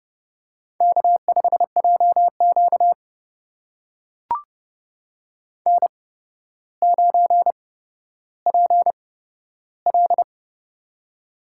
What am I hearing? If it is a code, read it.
Morse code, heard as a keyed tone, about 29 wpm, spaced as for 13 wpm, 700 Hz. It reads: K5JQ N9PL
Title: USA Call Signs - 4 Characters - 8x Character Spacing - 30wpm